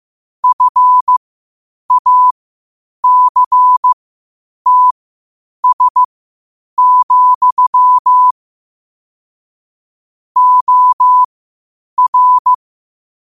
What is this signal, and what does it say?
Morse code, heard as a keyed tone, about 15 wpm, spaced as for 8 wpm, 1000 Hz. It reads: FACTS, OR
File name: qotdfiles / 8wpm